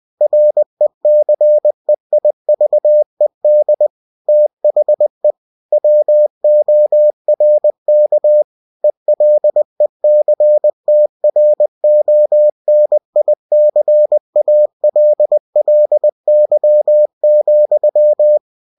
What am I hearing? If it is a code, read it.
Morse code, heard as a keyed tone, about 20 wpm, 600 Hz. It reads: RECEIVED THE WORK ELECTRONICALLY,
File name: cw-books / winnie-the-pooh / winnie-20wpm-page-0119